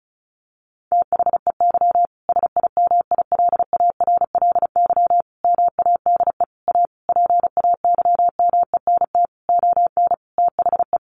Morse code, heard as a keyed tone, about 34 wpm, 700 Hz. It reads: THEY SIMILARLY MADE A PAYMENT ON THE
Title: Sentences from Top 1400 Words 35wpm